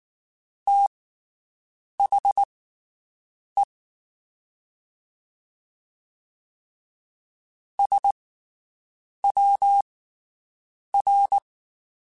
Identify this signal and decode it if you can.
Morse code, heard as a keyed tone, about 19 wpm, spaced as for 7 wpm, 800 Hz. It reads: THE SWR
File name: QSOs Spaced